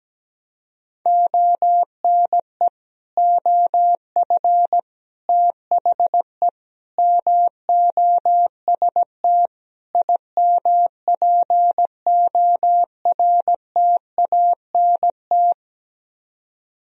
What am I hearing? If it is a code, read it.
Morse code, heard as a keyed tone, about 17 wpm, 700 Hz. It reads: ONE OF THE MOST IMPORTANT
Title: Speed-Racing - Sets of 5 Words 17wpm